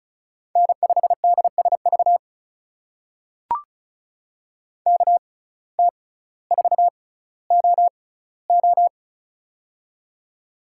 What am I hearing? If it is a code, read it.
Morse code, heard as a keyed tone, about 36 wpm, spaced as for 18 wpm, 700 Hz. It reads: N5DSV KT4OO
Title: USA Call Signs - 5 Characters - 6x Character Spacing - 35wpm